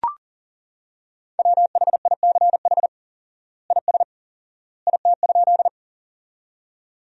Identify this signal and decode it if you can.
Morse code, heard as a keyed tone, about 40 wpm, 700 Hz. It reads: WHICH IS IT?
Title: 4x Word-Spacing - Sentences from Top 100 Words - 40wpm